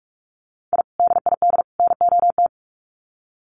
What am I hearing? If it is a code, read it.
Morse code, heard as a keyed tone, about 44 wpm, 700 Hz. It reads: I DID NOT
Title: Sets of 3 Words 45wpm